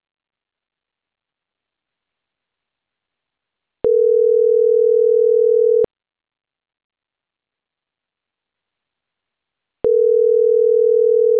Ringback tone (North America), 440 + 480 Hz.